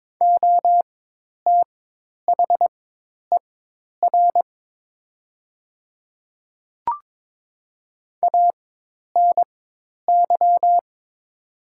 Morse code, heard as a keyed tone, about 22 wpm, spaced as for 10 wpm, 700 Hz. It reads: OTHER ANY
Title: 4x Character Spacing - Top 300 Words - 22wpm